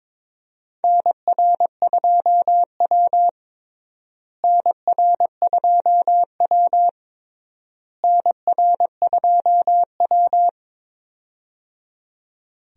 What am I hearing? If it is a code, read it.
Morse code, heard as a keyed tone, about 22 wpm, 700 Hz. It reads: NR2W NR2W NR2W
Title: USA Call Signs - 4 Characters - Repeated 3x - 22wpm